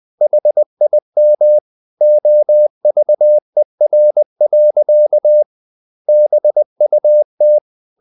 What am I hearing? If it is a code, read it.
Morse code, heard as a keyed tone, about 20 wpm, 600 Hz. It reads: HIM OVER. BUT
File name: the old man and the sea part 2 20wpm